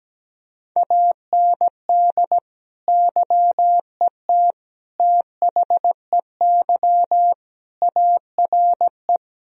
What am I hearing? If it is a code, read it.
Morse code, heard as a keyed tone, about 17 wpm, 700 Hz. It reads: AND YET THEY ARE
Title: Sentences from Top 1000 Words - Review Part 4 - 17wpm